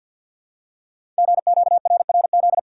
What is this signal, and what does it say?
Morse code, heard as a keyed tone, about 48 wpm, 700 Hz. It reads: GQRRZ
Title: Code Groups - 5 Letters - 50wpm